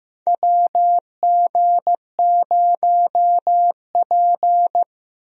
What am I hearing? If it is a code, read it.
Morse code, heard as a keyed tone, about 15 wpm, 700 Hz. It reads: WG0P
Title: USA Call Signs - 4 Characters - Repeated 3x - 15wpm